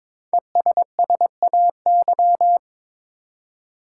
Morse code, heard as a keyed tone, about 21 wpm, 690 Hz. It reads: ESSAY
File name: 5 Letter Words - First 3 Letters Given - 22wpm